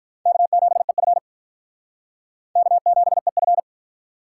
Morse code, heard as a keyed tone, about 44 wpm, 700 Hz. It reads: K7EF K7EF